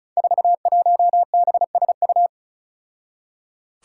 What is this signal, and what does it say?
Morse code, heard as a keyed tone, about 36 wpm, 700 Hz. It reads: 41BSU
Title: Five Challenging Letter-Number Combinations 35wpm